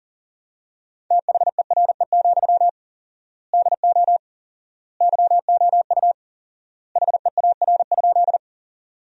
Morse code, heard as a keyed tone, about 37 wpm, 700 Hz. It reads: THERE, DO YOU HEAR?
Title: 4x Word-Spacing - Sentences from Top 300 Words - 40wpm